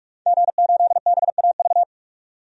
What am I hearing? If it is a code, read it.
Morse code, heard as a keyed tone, about 46 wpm, 700 Hz. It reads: G8BAV
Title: DXCC-100 Call Signs - 5 Characters - 45wpm